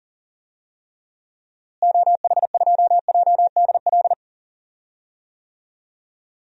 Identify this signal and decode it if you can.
Morse code, heard as a keyed tone, about 40 wpm, 700 Hz. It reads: OH2JDL
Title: DXCC-100 Call Signs - 6 Characters - Repeated 2x - 40wpm